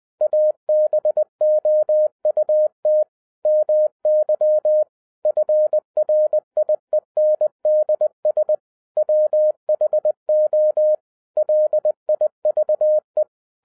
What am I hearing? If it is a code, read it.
Morse code, heard as a keyed tone, about 20 wpm, 610 Hz. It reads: ABOUT MY FRIENDS WHO LIVE